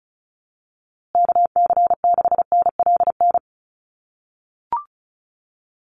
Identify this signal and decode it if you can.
Morse code, heard as a keyed tone, about 34 wpm, 700 Hz. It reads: KC6NLN
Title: USA Call Signs Repeated 3x 35wpm